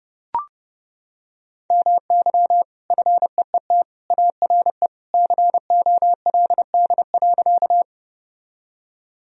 Morse code, heard as a keyed tone, about 32 wpm, 700 Hz. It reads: MY FEET ARE COLD.
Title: Sentences from Top 1000 Words - Review Part 1 - 30wpm